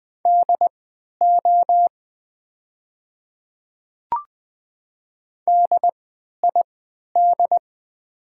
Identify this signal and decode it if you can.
Morse code, heard as a keyed tone, about 20 wpm, spaced as for 11 wpm, 700 Hz. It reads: DO DID
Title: 3x Character Spacing - Top 100 Words - 20wpm